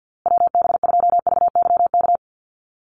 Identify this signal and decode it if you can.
Morse code, heard as a keyed tone, about 50 wpm, 700 Hz. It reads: WB2VYK
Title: Call Signs Repeated 2x - 50wpm